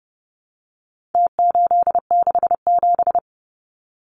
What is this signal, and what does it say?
Morse code, heard as a keyed tone, about 30 wpm, 700 Hz. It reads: T867
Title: Numbers - Contest Style - 4-Digit 30wpm